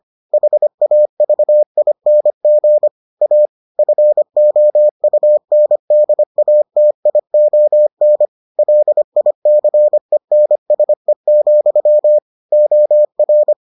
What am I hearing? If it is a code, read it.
Morse code, heard as a keyed tone, about 25 wpm, 600 Hz. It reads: HAVING A FOUNDATION LICENSE, OR